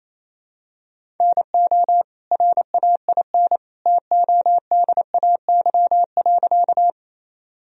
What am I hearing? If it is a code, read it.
Morse code, heard as a keyed tone, about 27 wpm, 700 Hz. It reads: NO RAIN TODAY.
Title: Sentences from Top 1000 Words - Review Part 1 - 28wpm